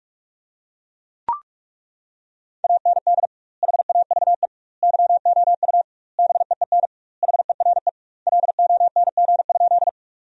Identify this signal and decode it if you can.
Morse code, heard as a keyed tone, about 41 wpm, 700 Hz. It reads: AND HAVE YOU BEEN HERE LONG?
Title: 2x Word-Spacing - Sentences from Top 200 Words - 45wpm